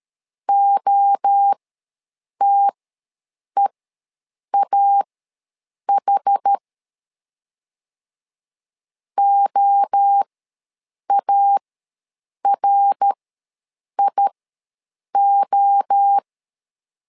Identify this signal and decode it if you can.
Morse code, heard as a keyed tone, about 13 wpm, spaced as for 7 wpm, 800 Hz. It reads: OTEAH OARIO
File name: LEZIONI / 3 MWRO / 3 MWRO ws 12wpm 1914